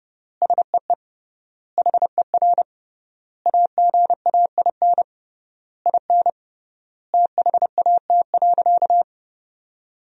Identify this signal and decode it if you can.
Morse code, heard as a keyed tone, about 30 wpm, 700 Hz. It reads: SEE HER AGAIN IN THAT.
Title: 3x Word-Spacing - Sentences from Top 200 Words - 30wpm